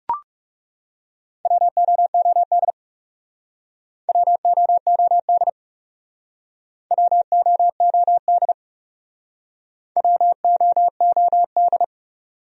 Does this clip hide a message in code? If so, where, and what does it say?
Yes: Morse code from 1.5 to 11.8 s, heard as a keyed tone, about 38 wpm, 700 Hz. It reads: WOOD WOOD WOOD WOOD